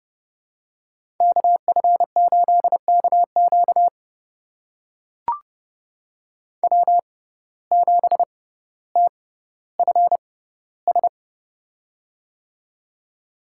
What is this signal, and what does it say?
Morse code, heard as a keyed tone, about 31 wpm, spaced as for 15 wpm, 700 Hz. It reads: KF8KQ W7TFS
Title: USA Call Signs - 5 Characters - 6x Character Spacing - 30wpm